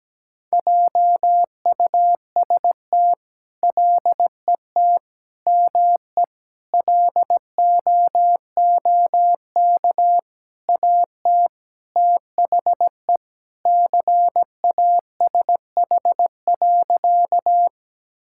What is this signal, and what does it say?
Morse code, heard as a keyed tone, about 17 wpm, 700 Hz. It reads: JUST LET ME LOOK AT THE CASH.
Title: Sentences from Top 1000 Words - Review Part 5 - 17wpm